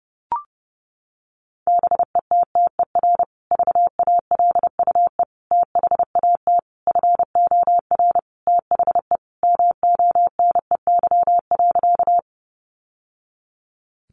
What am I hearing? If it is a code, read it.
Morse code, heard as a keyed tone, about 30 wpm, 700 Hz. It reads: BETTER VALUE THAT FOR THE MONEY.
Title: Sentences from Top 600 Words 30wpm